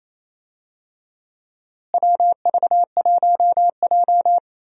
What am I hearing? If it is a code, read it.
Morse code, heard as a keyed tone, about 28 wpm, 700 Hz. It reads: WV1J
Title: USA Call Signs - 4 Characters - Repeated 2x - 28wpm